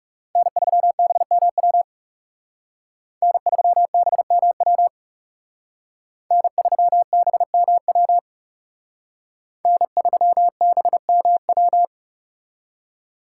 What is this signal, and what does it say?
Morse code, heard as a keyed tone, about 38 wpm, 700 Hz. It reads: N3BMW N3BMW N3BMW N3BMW